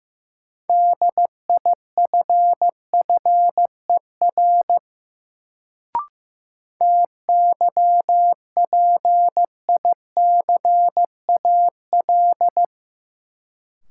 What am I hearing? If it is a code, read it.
Morse code, heard as a keyed tone, about 15 wpm, 700 Hz. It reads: DIFFER TYPICAL
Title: Sentences from Top 1100 Words - New Word Drill Alt - 15wpm